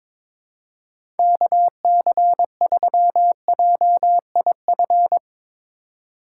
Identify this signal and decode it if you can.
Morse code, heard as a keyed tone, about 22 wpm, 700 Hz. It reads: KC3JIF
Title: USA Call Signs - 6 Characters - 3x Character Spacing - 22wpm